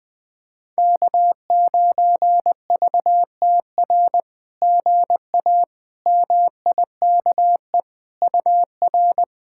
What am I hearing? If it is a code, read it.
Morse code, heard as a keyed tone, about 20 wpm, 700 Hz. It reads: K9VTR GA MIKE UR
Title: SOTA QSO Practice 20wpm